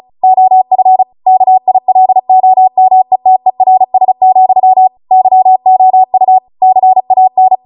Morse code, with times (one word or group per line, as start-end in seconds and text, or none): 0.2-1.0 OF
1.3-4.9 KILOMETERS,
5.1-6.4 YOU
6.6-7.5 CAN